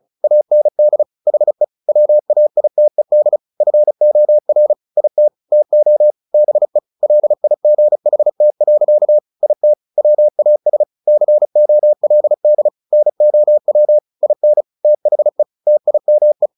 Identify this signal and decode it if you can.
Morse code, heard as a keyed tone, about 35 wpm, 600 Hz. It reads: AND HE WAITED FOR IT TO BE LIGHT. IT WAS COLD NOW IN THE TIME
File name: the old man and the sea / the old man and the sea 1 35wpm